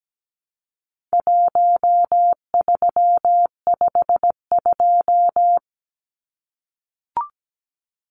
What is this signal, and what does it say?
Morse code, heard as a keyed tone, about 17 wpm, 700 Hz. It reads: 1352
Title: Numbers - Contest Style - 4-Digit 17wpm